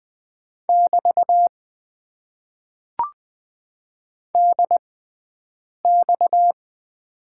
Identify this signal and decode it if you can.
Morse code, heard as a keyed tone, about 20 wpm, spaced as for 7 wpm, 700 Hz. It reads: <BT> DX
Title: Top 90 QSO Elements - 6x Character Spacing - 20wpm